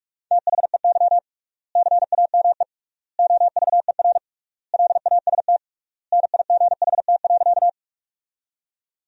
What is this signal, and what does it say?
Morse code, heard as a keyed tone, about 48 wpm, 700 Hz. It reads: THEY CAME OVER LAST NIGHT.
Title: 3x Word-Spacing - Sentences from Top 300 Words - 45wpm